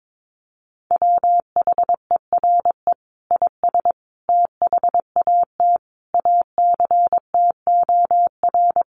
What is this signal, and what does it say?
Morse code, heard as a keyed tone, about 22 wpm, 700 Hz. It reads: WHERE IS THAT ACTOR
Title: Sentences from Top 1500 Words 22wpm